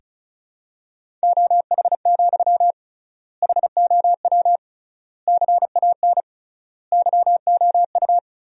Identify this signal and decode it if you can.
Morse code, heard as a keyed tone, about 34 wpm, 700 Hz. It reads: OH, HOW CAN YOU
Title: 3x Word-Spacing - Sentences from Top 500 Words - 35wpm